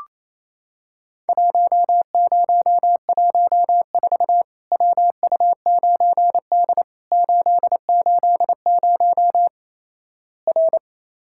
Morse code, heard as a keyed tone, about 28 wpm, 700 Hz. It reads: 1014 WU9D 880 R
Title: Worked All Europe DX Contest - QTC Exchange - 28wpm